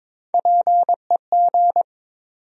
Morse code, heard as a keyed tone, about 22 wpm, 700 Hz. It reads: PEG
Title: USA Common Names - Max 3 Letters - 22wpm